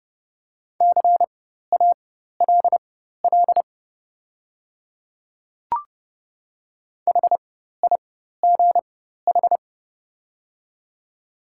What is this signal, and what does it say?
Morse code, heard as a keyed tone, about 30 wpm, spaced as for 14 wpm, 690 Hz. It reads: CALL HIGH
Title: 4x Character Spacing - Top 500 Words - 30wpm